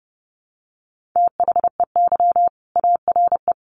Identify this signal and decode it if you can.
Morse code, heard as a keyed tone, about 30 wpm, 700 Hz. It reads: THEY ARE
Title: Sets of 2 Words 30wpm